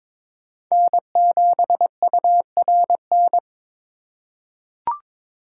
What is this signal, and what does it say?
Morse code, heard as a keyed tone, about 22 wpm, 700 Hz. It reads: N7URN
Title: USA Call Signs - 5 Characters - 3x Character Spacing - 22wpm